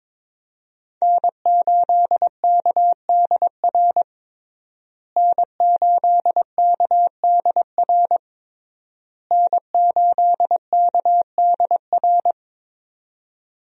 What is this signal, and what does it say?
Morse code, heard as a keyed tone, about 22 wpm, 700 Hz. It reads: N8KDR N8KDR N8KDR
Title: USA Call Signs - 5 Characters - Repeated 3x - 22wpm